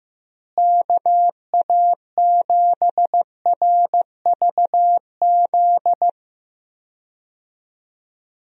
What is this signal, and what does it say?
Morse code, heard as a keyed tone, about 15 wpm, 700 Hz. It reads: KA7RVZ
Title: USA Call Signs - 6 Characters - Repeated 2x - 15wpm